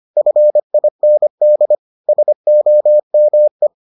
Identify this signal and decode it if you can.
Morse code, heard as a keyed tone, about 25 wpm, 600 Hz. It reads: FIND SOME